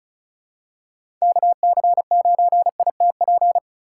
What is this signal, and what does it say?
Morse code, heard as a keyed tone, about 36 wpm, 700 Hz. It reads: KC9ITP
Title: USA Call Signs - 6 Characters - 2x Character Spacing - 35wpm